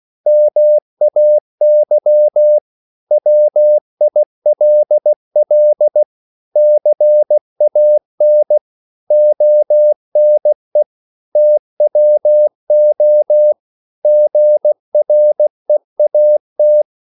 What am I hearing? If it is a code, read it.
Morse code, heard as a keyed tone, about 16 wpm, 600 Hz. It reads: MAY WILL CAN ONE TWO GREAT
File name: CW / Practice / 500MostCommonEnglishWords16